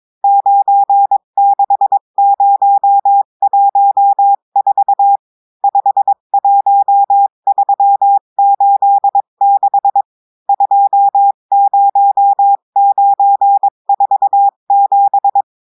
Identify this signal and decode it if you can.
Morse code, heard as a keyed tone, about 22 wpm, 800 Hz. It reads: 96014 51386 20947
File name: cours-du-mercredi / audios / cours-853-du-2-mai-2024-etxte chiffre et texte clair-a-20-wpm